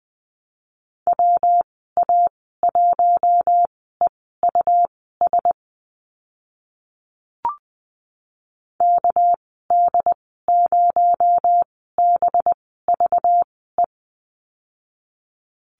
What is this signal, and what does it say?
Morse code, heard as a keyed tone, about 20 wpm, spaced as for 15 wpm, 700 Hz. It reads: WA1EUS KD0BVE
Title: 238 Lesson 38g - Letter q - Call Signs 20wpm